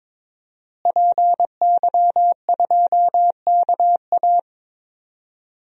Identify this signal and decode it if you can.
Morse code, heard as a keyed tone, about 22 wpm, 700 Hz. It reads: PY2KA
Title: DXCC-100 Call Signs - 5 Characters - Repeated 2x - 22wpm